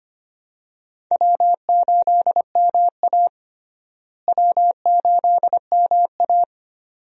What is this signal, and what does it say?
Morse code, heard as a keyed tone, about 25 wpm, 700 Hz. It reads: W8MA W8MA
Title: USA Call Signs - 4 Characters - Repeated 2x - 25wpm